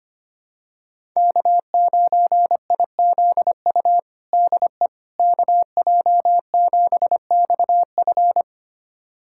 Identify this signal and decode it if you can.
Morse code, heard as a keyed tone, about 25 wpm, 700 Hz. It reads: K9IZU DE KJ7XF
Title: USA Callsign DE Callsign - 5 Characters - 25wpm